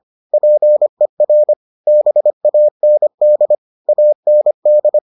Morse code, heard as a keyed tone, about 25 wpm, 600 Hz. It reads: PER BAND AND